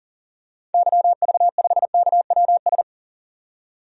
Morse code, heard as a keyed tone, about 40 wpm, 700 Hz. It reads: YV5KWS
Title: DXCC-100 Call Signs - 6 Characters - 40wpm